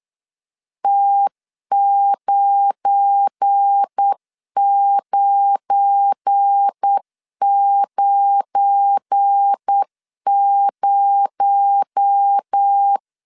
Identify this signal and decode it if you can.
Morse code, heard as a keyed tone, about 8 wpm, 800 Hz. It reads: T9990
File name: LEZIONI / 7 GF9Z / 7 GF9 8wpm 3545